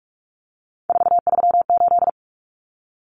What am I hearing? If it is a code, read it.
Morse code, heard as a keyed tone, about 44 wpm, 700 Hz. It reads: 438
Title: Common Cut Numbers - 3-Digit 45wpm